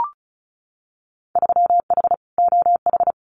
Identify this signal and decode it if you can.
Morse code, heard as a keyed tone, about 34 wpm, 700 Hz. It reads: 3H OH
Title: Winter Field Day Exchanges Encore 35wpm